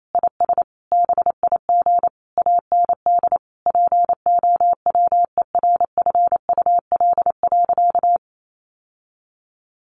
Morse code, heard as a keyed tone, about 28 wpm, 700 Hz. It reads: IS BIG AND POWERFUL.
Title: Sentences from Top 1300 Words 28wpm